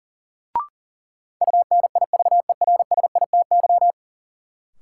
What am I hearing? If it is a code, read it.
Morse code, heard as a keyed tone, about 40 wpm, 700 Hz. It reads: UNIVERSITY
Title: Sentences from Top 1300 Words - New Word Drill Alt - 40wpm